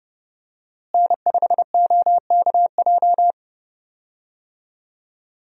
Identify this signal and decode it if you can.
Morse code, heard as a keyed tone, about 30 wpm, 700 Hz. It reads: N5OKJ
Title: USA Call Signs - 5 Characters - Repeated 2x - 30wpm